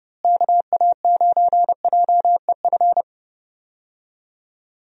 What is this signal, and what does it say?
Morse code, heard as a keyed tone, about 30 wpm, 700 Hz. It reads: KA9JEF
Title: USA Call Signs - 6 Characters - Repeated 2x - 30wpm